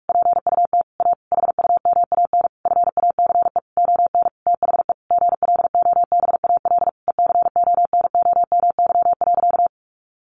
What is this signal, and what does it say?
Morse code, heard as a keyed tone, about 44 wpm, 700 Hz. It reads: PUT A HUMAN FACE ON THE GLOBAL ECONOMY.
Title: Sentences from Top 1200 Words 45wpm